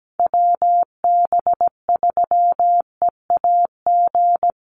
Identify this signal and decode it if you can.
Morse code, heard as a keyed tone, about 17 wpm, 700 Hz. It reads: WB3EAG
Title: USA Call Signs Repeated 3x 17wpm